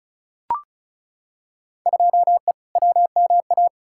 Morse code, heard as a keyed tone, about 34 wpm, 700 Hz. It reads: 2E WMA
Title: Field Day Exchanges Encore - 35wpm